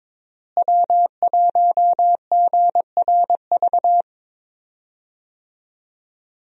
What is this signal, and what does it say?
Morse code, heard as a keyed tone, about 22 wpm, 700 Hz. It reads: W1GRV
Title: USA Call Signs - 5 Characters - Repeated 2x - 22wpm